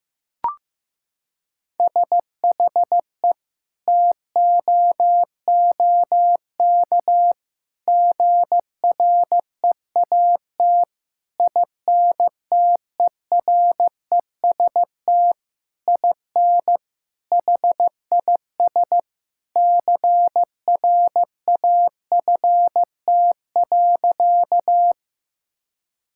Morse code, heard as a keyed tone, about 15 wpm, 700 Hz. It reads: SHE TOOK GREAT INTEREST IN HIS CRAFT.
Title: Sentences from Top 1000 Words - Review Part 6 - 15wpm